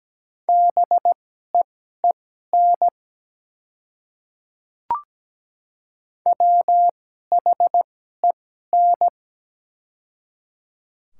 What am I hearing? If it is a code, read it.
Morse code, heard as a keyed tone, about 17 wpm, spaced as for 12 wpm, 700 Hz. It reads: BEEN WHEN
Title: 2x Character Spacing - Top 100 Words - 17wpm